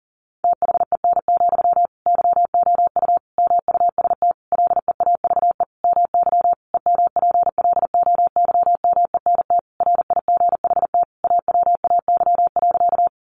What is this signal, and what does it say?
Morse code, heard as a keyed tone, about 37 wpm, 700 Hz. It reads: THEN, YOU MUST LEAVE MY EMPLOYMENT RIGHT AWAY.
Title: Sentences from Top 1500 Words 40wpm